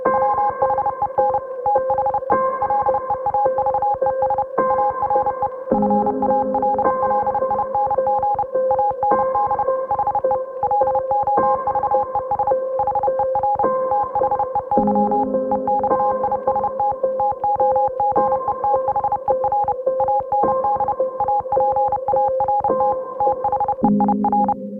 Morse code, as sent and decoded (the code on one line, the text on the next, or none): .-- .... -. / - .... / .-.. .- ...- . ... / .- .-. / ..-. .- .-.. .-.. .. -. --. / .- -. -.. / .... / .-.. --- ...- . ... / .... .-. / - .... -.-- / . -..- .. ... - / - --- --. - .... .-. / .- -. -.. / .- .--. .- .-. - / - .... .. .-.
WHEN THE LEAVES ARE FALLING AND HE LOVES HER THEY EXIST TOGETHER AND APART THEIR